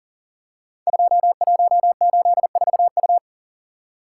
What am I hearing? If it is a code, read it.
Morse code, heard as a keyed tone, about 41 wpm, 700 Hz. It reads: 2184U
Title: Code Groups - 5 Letter:Numbers - 40wpm